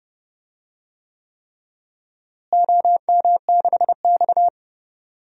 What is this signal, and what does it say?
Morse code, heard as a keyed tone, about 30 wpm, 700 Hz. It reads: OM6X
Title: DXCC-100 Call Signs - 4 Characters - Repeated 3x - 30wpm